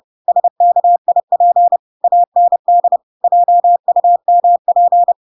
Morse code, heard as a keyed tone, about 30 wpm, 700 Hz. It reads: SKIP AND JUMP
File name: Pinocchio / Pinocchio ch8 30 30 ik1hge